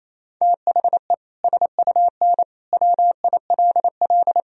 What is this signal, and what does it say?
Morse code, heard as a keyed tone, about 28 wpm, 700 Hz. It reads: THE SUN WILL
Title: Sentences from Top 1000 Words - Review Part 3 - 28wpm